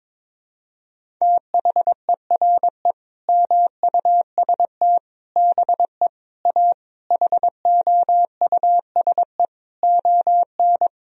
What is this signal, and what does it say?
Morse code, heard as a keyed tone, about 22 wpm, 700 Hz. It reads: THERE MUST BE A HOUSE ON
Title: Sentences from Top 1000 Words - Review Part 1 - 22wpm